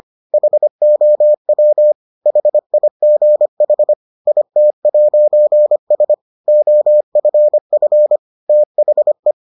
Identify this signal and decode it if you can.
Morse code, heard as a keyed tone, about 25 wpm, 600 Hz. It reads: HOW HIGH IT'S OFF THE